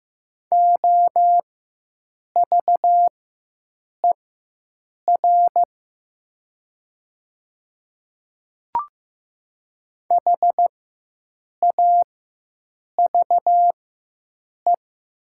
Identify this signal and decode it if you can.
Morse code, heard as a keyed tone, about 15 wpm, spaced as for 7 wpm, 700 Hz. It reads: OVER HAVE
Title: 4x Character Spacing - Top 100 Words - 15wpm